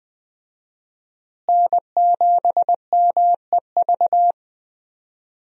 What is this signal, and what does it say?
Morse code, heard as a keyed tone, about 20 wpm, 700 Hz. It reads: N7MEV
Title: USA Call Signs - 5 Characters - Repeated 2x - 20wpm